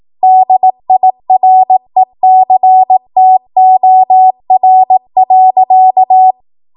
Morse code, heard as a keyed tone, about 18 wpm, 750 Hz. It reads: DIRECTOR.